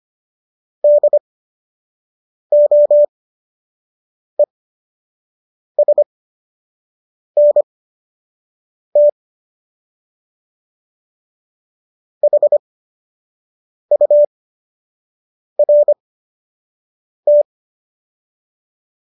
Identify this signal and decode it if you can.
Morse code, heard as a keyed tone, about 25 wpm, spaced as for 6 wpm, 590 Hz. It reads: DOESNT HURT